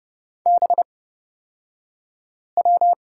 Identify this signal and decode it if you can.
Morse code, heard as a keyed tone, about 29 wpm, spaced as for 5 wpm, 700 Hz. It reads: BW
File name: Lesson 38d - Letter q - Two Characters 30wpm